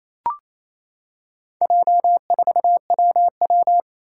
Morse code, heard as a keyed tone, about 28 wpm, 700 Hz. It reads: J4WW
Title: Four Challenging Letter-Number Combinations 28wpm